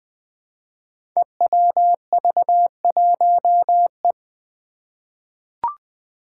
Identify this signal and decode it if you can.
Morse code, heard as a keyed tone, about 20 wpm, 700 Hz. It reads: EWV1E
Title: Five Challenging Letter-Number Combinations 20wpm